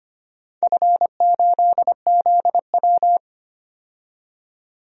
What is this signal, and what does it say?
Morse code, heard as a keyed tone, about 25 wpm, 700 Hz. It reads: F8ZW
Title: DXCC-100 Call Signs - 4 Characters - 25wpm